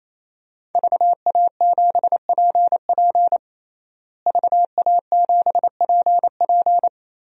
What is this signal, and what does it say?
Morse code, heard as a keyed tone, about 28 wpm, 700 Hz. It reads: VA7PP VA7PP
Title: DXCC-100 Call Signs - 5 Characters - Repeated 3x - 28wpm